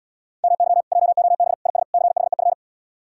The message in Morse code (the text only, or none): MY JOB IS CHEF